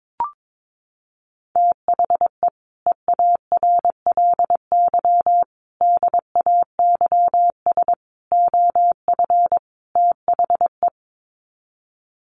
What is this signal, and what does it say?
Morse code, heard as a keyed tone, about 22 wpm, 700 Hz. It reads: THE EARLY DAYS OF THE
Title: Sets of 5 Words Encore 22wpm